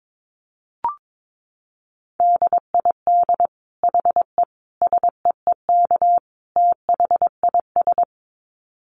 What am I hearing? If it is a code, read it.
Morse code, heard as a keyed tone, about 22 wpm, 700 Hz. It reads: DID HE SEEK THIS
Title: Dit Challenge 22wpm